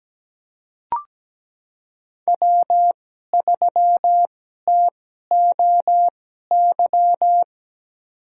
Morse code, heard as a keyed tone, about 17 wpm, spaced as for 12 wpm, 700 Hz. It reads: W3TOY
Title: USA Call Signs - 2x Character Spacing - 17wpm